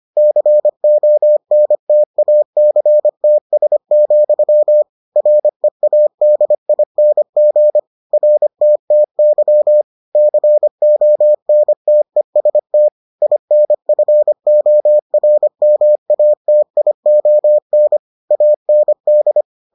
Morse code, heard as a keyed tone, about 25 wpm, 600 Hz. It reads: CONTACTS, READING RTTY CONTEST INFORMATION AND